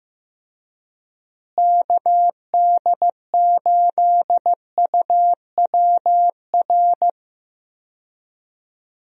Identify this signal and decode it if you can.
Morse code, heard as a keyed tone, about 15 wpm, 700 Hz. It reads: KD8UWR
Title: USA Call Signs - 6 Characters - Repeated 2x - 15wpm